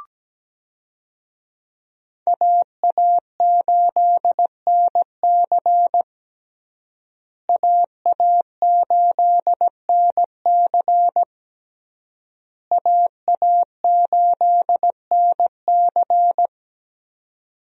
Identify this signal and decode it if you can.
Morse code, heard as a keyed tone, about 17 wpm, 700 Hz. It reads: AA8NC AA8NC AA8NC
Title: USA Call Signs - 5 Characters - Repeated 3x - 17wpm